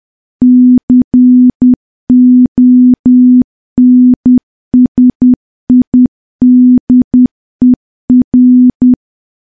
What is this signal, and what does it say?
Morse code, heard as a keyed tone, about 10 wpm, 250 Hz. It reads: CONSIDER
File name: morse / morse-training-10WPM